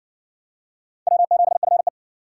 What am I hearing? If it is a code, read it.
Morse code, heard as a keyed tone, about 52 wpm, 700 Hz. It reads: W7FE